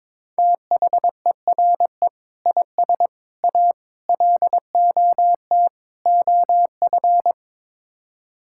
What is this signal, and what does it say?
Morse code, heard as a keyed tone, about 22 wpm, 700 Hz. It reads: THERE IS A LOT OF
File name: practice / 22wpm / Build A Phrase - 5-Word Phrases - 22wpm